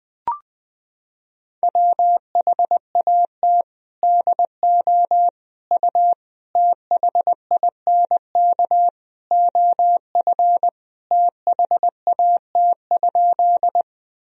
Morse code, heard as a keyed tone, about 21 wpm, 700 Hz. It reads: WHAT DO U THINK OF THAT?
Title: Sentences from Top 200 Words and Abbreviations 20wpm